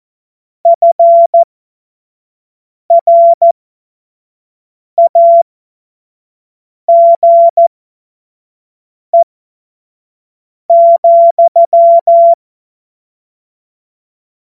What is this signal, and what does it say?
Morse code, heard as a keyed tone, about 14 wpm, spaced as for 5 wpm, 680 Hz. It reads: FRAGE,
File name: Morsekurs / Witze / Witz 06. September 2022 25 F70